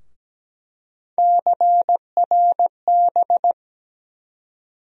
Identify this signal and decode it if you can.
Morse code, heard as a keyed tone, about 17 wpm, 700 Hz. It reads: CRB
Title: USA Call Sign Postfixes - 3 Letters 17wpm